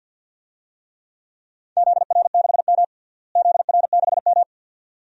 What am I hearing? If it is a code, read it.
Morse code, heard as a keyed tone, about 52 wpm, 700 Hz. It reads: ZR6M ZR6M
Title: DXCC-100 Call Signs - 4 Characters - Repeated 2x - 50wpm